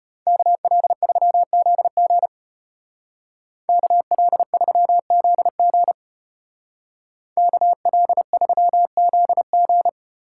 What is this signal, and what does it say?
Morse code, heard as a keyed tone, about 34 wpm, 700 Hz. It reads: KL3ZG KL3ZG KL3ZG